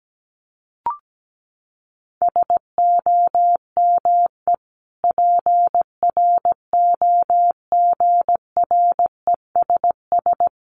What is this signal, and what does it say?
Morse code, heard as a keyed tone, about 17 wpm, 700 Hz. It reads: SOME PROGRESS
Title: Sentences from Top 1400 Words 17wpm